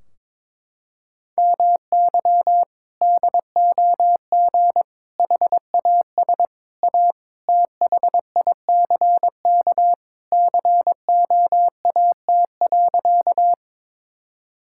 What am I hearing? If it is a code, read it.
Morse code, heard as a keyed tone, about 22 wpm, 700 Hz. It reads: MY DOG HAS A THICK COAT.
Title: Sentences from Top 1600 Words 22wpm